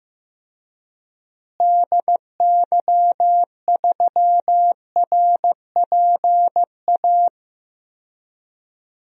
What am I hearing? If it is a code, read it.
Morse code, heard as a keyed tone, about 16 wpm, 700 Hz. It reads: DY3RPA
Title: DXCC-100 Call Signs - 6 Characters - Repeated 2x - 15wpm